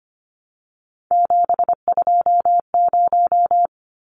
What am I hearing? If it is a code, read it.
Morse code, heard as a keyed tone, about 25 wpm, 700 Hz. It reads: 720
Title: Numbers - 3-Digit 25wpm